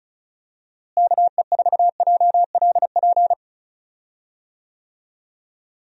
Morse code, heard as a keyed tone, about 36 wpm, 700 Hz. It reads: KE4JLP
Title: USA Call Signs - 6 Characters - Repeated 3x - 35wpm